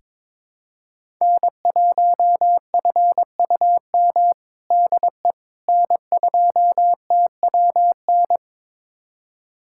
Morse code, heard as a keyed tone, about 22 wpm, 700 Hz. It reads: N1FUM DE N2TWN
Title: USA Callsign DE Callsign - 5 Characters - 22wpm